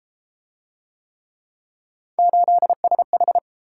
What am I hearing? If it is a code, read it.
Morse code, heard as a keyed tone, about 30 wpm, 700 Hz. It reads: 8SH